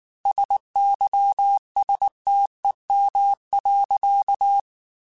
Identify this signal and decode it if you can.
Morse code, heard as a keyed tone, about 19 wpm, 800 Hz. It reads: SYSTEM.
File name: Technical Subjects / Technical Subjects-mp3s / Sw Pwr Sup @18